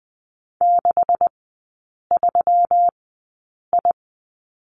Morse code, heard as a keyed tone, about 20 wpm, spaced as for 8 wpm, 690 Hz. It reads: 63I
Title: 133 Lesson 22e - Question Mark - Three Characters 20wpm